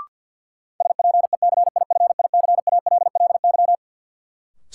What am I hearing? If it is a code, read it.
Morse code, heard as a keyed tone, about 52 wpm, 700 Hz. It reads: SPECIFICALLY
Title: Sentences from Top 1100 Words - New Word Drill Alt - 50wpm